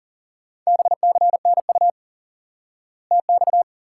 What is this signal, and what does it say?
Morse code, heard as a keyed tone, about 40 wpm, 700 Hz. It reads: BCNU TX
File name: IQE Course 3 - Lesson 89 - YAGI - Two Elements - 40wpm